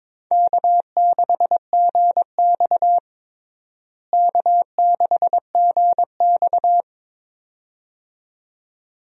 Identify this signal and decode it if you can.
Morse code, heard as a keyed tone, about 22 wpm, 700 Hz. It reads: K6GX K6GX
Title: USA Call Signs - 4 Characters - Repeated 2x - 22wpm